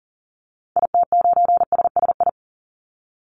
Morse code, heard as a keyed tone, about 40 wpm, 700 Hz. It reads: IT9SSI
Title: Call Signs Repeated 2x - 40wpm